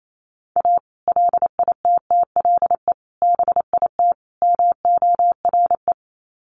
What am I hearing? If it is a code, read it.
Morse code, heard as a keyed tone, about 28 wpm, 700 Hz. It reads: A LITTLE BIT MORE
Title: Sets of 4 Words 28wpm